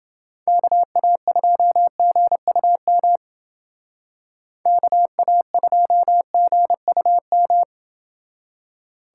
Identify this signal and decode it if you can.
Morse code, heard as a keyed tone, about 30 wpm, 700 Hz. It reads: KA2GUM KA2GUM